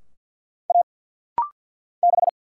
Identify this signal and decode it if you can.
Morse code, heard as a keyed tone, about 50 wpm, 700 Hz. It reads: A 6
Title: Single Letter-Number Alt - Rapid-Fire 50wpm